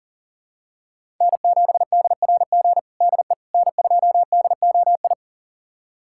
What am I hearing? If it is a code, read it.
Morse code, heard as a keyed tone, about 40 wpm, 700 Hz. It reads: N7DRG DE N2DOI
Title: USA Callsign DE Callsign - 5 Characters - 40wpm